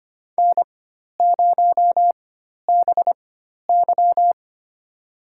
Morse code, heard as a keyed tone, about 25 wpm, spaced as for 12 wpm, 700 Hz. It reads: N0BY
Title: USA Call Signs - 4 Characters - 4x Character Spacing - 25wpm